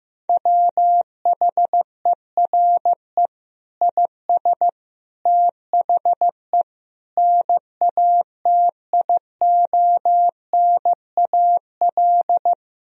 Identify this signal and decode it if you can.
Morse code, heard as a keyed tone, about 15 wpm, 700 Hz. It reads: WHERE IS THE NATIONAL
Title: Sentences from Top 1000 Words - Review Part 4 - 15wpm